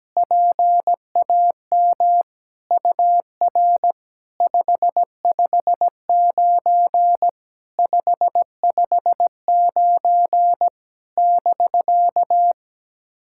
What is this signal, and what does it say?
Morse code, heard as a keyed tone, about 17 wpm, 700 Hz. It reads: PAM UR 559 559 <BK>
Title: SOTA QSO Practice 17wpm